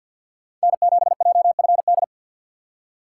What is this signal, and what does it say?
Morse code, heard as a keyed tone, about 52 wpm, 700 Hz. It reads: N7JFD
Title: USA Call Signs - 5 Characters - 6x Character Spacing - 50wpm